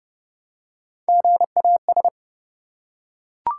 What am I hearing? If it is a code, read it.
Morse code, heard as a keyed tone, about 30 wpm, 700 Hz. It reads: GAS
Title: Three Letter Words - 30wpm